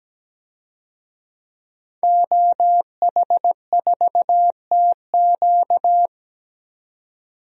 Morse code, heard as a keyed tone, about 17 wpm, 700 Hz. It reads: OH4TQ
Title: DXCC-100 Call Signs - 5 Characters - 17wpm